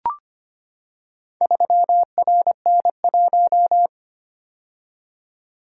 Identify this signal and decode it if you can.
Morse code, heard as a keyed tone, about 25 wpm, 700 Hz. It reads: 3RN1
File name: practice / 25wpm / Four Letter-Number Combinations - 25wpm